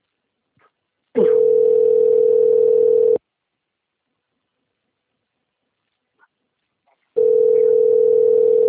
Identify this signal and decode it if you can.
Ringback tone (North America), 440 + 480 Hz.